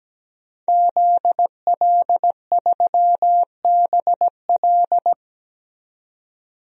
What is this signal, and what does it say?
Morse code, heard as a keyed tone, about 17 wpm, 700 Hz. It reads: ZL3BL
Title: DXCC-100 Call Signs - 5 Characters - 17wpm